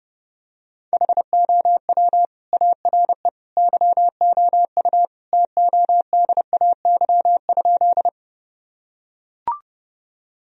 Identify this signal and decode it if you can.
Morse code, heard as a keyed tone, about 28 wpm, 700 Hz. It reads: HOW ARE YOU TODAY?
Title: Sentences from Top 1000 Words - Review Part 4 - 30wpm